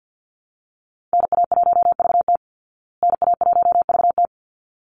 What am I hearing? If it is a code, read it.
Morse code, heard as a keyed tone, about 50 wpm, 700 Hz. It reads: NA1VT NA1VT
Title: Call Signs Repeated 3x - 50wpm